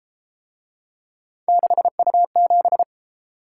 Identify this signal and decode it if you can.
Morse code, heard as a keyed tone, about 33 wpm, 700 Hz. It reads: 6U7